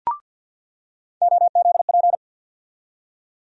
Morse code, heard as a keyed tone, about 53 wpm, 700 Hz. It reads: OZP
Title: USA Call Sign Postfixes - 3 Letters 50wpm